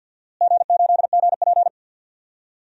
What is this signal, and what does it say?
Morse code, heard as a keyed tone, about 46 wpm, 700 Hz. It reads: G8GP
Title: DXCC-100 Call Signs - 4 Characters - Repeated 2x - 50wpm